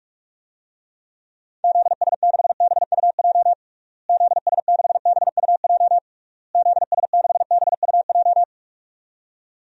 Morse code, heard as a keyed tone, about 46 wpm, 700 Hz. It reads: ZS6BUJ ZS6BUJ ZS6BUJ
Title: DXCC-100 Call Signs - 6 Characters - Repeated 3x - 45wpm